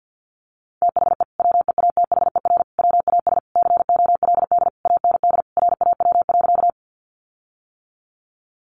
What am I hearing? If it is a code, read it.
Morse code, heard as a keyed tone, about 50 wpm, 700 Hz. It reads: THE WEATHER WAS COLD AND RAW.
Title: Sentences from Top 1500 Words 50wpm